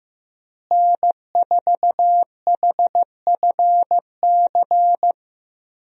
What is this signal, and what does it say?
Morse code, heard as a keyed tone, about 16 wpm, 700 Hz. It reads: N4HFC
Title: USA Call Signs - 5 Characters - Repeated 3x - 15wpm